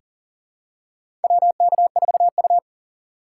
Morse code, heard as a keyed tone, about 40 wpm, 700 Hz. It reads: WK4U
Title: USA Call Signs - 4 Characters - 2x Character Spacing - 40wpm